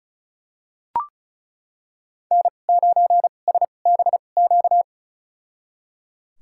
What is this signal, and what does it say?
Morse code, heard as a keyed tone, about 34 wpm, spaced as for 25 wpm, 700 Hz. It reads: N9SBQ
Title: USA Call Signs - 5 Characters - 2x Character Spacing - 35wpm